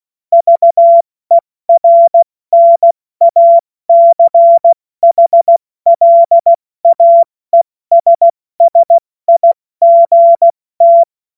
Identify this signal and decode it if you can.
Morse code, heard as a keyed tone, about 16 wpm, 680 Hz. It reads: VERNACHLAESSIGT
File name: Morsekurs / Witze / Witz 21. Juni 2022 70 F80